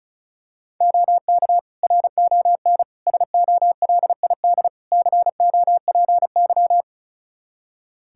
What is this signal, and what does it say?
Morse code, heard as a keyed tone, about 34 wpm, 700 Hz. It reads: OK RON SOLID COPY
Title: Practice QSO - Exchange 3 - More Info - 35wpm